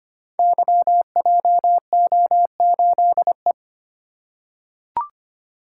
Morse code, heard as a keyed tone, about 25 wpm, 700 Hz. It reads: YJO8E
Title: Code Groups - 5 Letter:Numbers - 25wpm